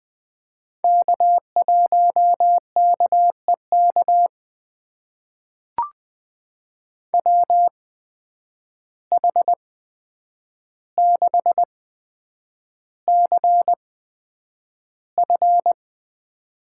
Morse code, heard as a keyed tone, about 20 wpm, spaced as for 9 wpm, 700 Hz. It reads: K1KEK WH6CF
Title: USA Call Signs - 5 Characters - 8x Character Spacing - 20wpm